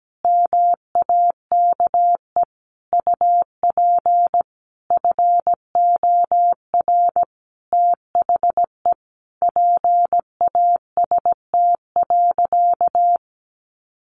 Morse code, heard as a keyed tone, about 17 wpm, 700 Hz. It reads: MAKE UP FOR THE PAST.
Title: Sentences from Top 1100 Words 17wpm